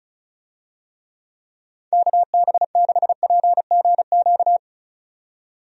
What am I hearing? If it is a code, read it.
Morse code, heard as a keyed tone, about 36 wpm, 700 Hz. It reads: KB6PGQ
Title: USA Call Signs - 6 Characters - 35wpm